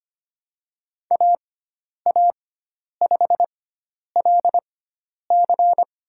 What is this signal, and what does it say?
Morse code, heard as a keyed tone, about 25 wpm, spaced as for 10 wpm, 700 Hz. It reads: AA5LC
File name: Lesson 16g - Letter c - Call Signs 25wpm